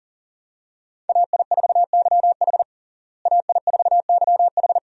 Morse code, heard as a keyed tone, about 40 wpm, 700 Hz. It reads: AI4YH AI4YH
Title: USA Call Signs - 5 Characters - Repeated 2x - 40wpm